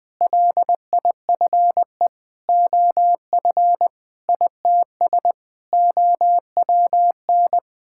Morse code, heard as a keyed tone, about 21 wpm, 700 Hz. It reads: LIFE OF ITS OWN
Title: Sets of 8 Words - 20wpm